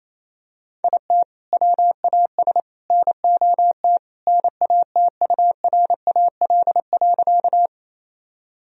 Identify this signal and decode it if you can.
Morse code, heard as a keyed tone, about 28 wpm, 700 Hz. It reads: IT WAS NOT NATURAL.
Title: Sentences from Top 1000 Words - Review Part 4 - 28wpm